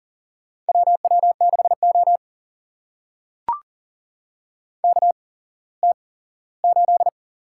Morse code, heard as a keyed tone, about 40 wpm, spaced as for 19 wpm, 700 Hz. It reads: WW6O KT8
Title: USA Call Signs - 4 Characters - 8x Character Spacing - 40wpm